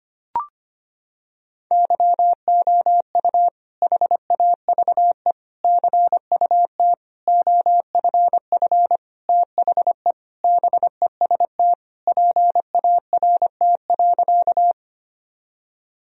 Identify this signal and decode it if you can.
Morse code, heard as a keyed tone, about 25 wpm, 700 Hz. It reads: YOU HAVE CUT OFF THE BEST PART.
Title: Sentences from Top 1000 Words - Review Part 1 - 25wpm